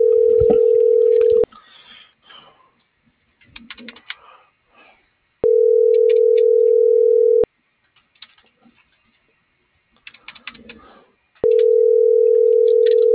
Ringback tone (North America), 440 + 480 Hz.